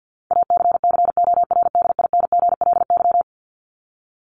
Morse code, heard as a keyed tone, about 50 wpm, 700 Hz. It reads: ACCORDINGLY
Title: LY Suffix - Top 100 Words with Base Words 50wpm